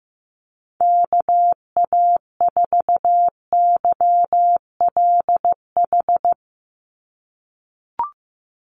Morse code, heard as a keyed tone, about 15 wpm, 700 Hz. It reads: KA4YLH
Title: USA Call Signs 15wpm